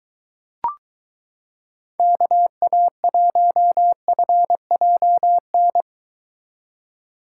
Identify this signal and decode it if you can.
Morse code, heard as a keyed tone, about 22 wpm, 700 Hz. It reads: KA1FJN